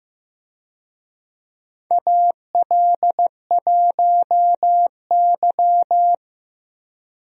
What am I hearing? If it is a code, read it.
Morse code, heard as a keyed tone, about 16 wpm, 700 Hz. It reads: AL1Y
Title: USA Call Signs - 4 Characters - Repeated 3x - 15wpm